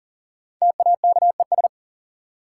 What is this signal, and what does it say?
Morse code, heard as a keyed tone, about 40 wpm, 700 Hz. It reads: TAKES
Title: Sentences from Top 1000 Words - Review Part 6 - New Word Drill - 40wpm